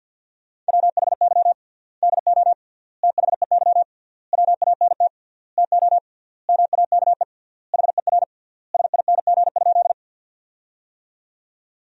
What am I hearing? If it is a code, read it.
Morse code, heard as a keyed tone, about 50 wpm, 700 Hz. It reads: WHY DO THEY WANT TO MAKE HER SING?
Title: 3x Word-Spacing - Sentences from Top 400 Words - 50wpm